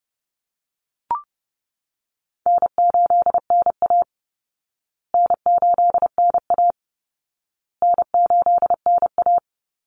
Morse code, heard as a keyed tone, about 30 wpm, 700 Hz. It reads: N8NA N8NA N8NA
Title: Call Signs Repeated 2x - 30wpm